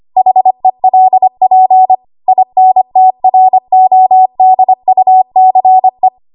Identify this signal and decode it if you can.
Morse code, heard as a keyed tone, about 25 wpm, 750 Hz. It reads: HELP INTRODUCE
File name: Morse / 25WPM